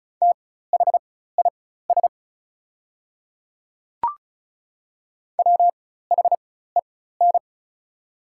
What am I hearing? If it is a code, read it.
Morse code, heard as a keyed tone, about 36 wpm, spaced as for 16 wpm, 700 Hz. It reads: THIS WHEN
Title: 4x Character Spacing - Top 100 Words - 35wpm